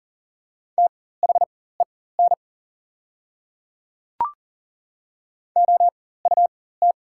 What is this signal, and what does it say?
Morse code, heard as a keyed tone, about 40 wpm, spaced as for 19 wpm, 700 Hz. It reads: THEN OUT
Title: 4x Character Spacing - Top 100 Words - 40wpm